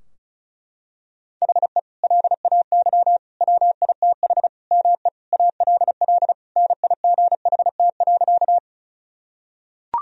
Morse code, heard as a keyed tone, about 36 wpm, 700 Hz. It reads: HE LAY WITH ME ALL NIGHT.
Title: Sentences from Top 1000 Words - Review Part 2 - 35wpm